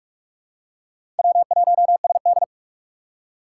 Morse code, heard as a keyed tone, about 46 wpm, 700 Hz. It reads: W1SD
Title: USA Call Signs - 2x Character Spacing - 45wpm